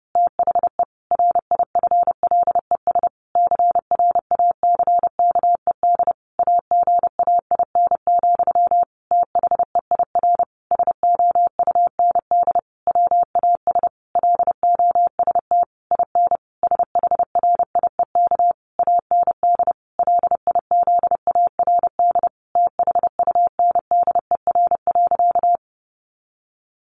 Morse code, heard as a keyed tone, about 30 wpm, 700 Hz. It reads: THE RIFLES CRACKED AGAIN, THEIR SOUND WAS LOST IN SHRIEK AND LIZARD THUNDER.
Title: A Sound Of Thunder 30wpm